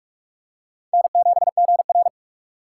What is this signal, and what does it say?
Morse code, heard as a keyed tone, about 43 wpm, 700 Hz. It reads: N7GR
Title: USA Call Signs - 4 Characters - 2x Character Spacing - 45wpm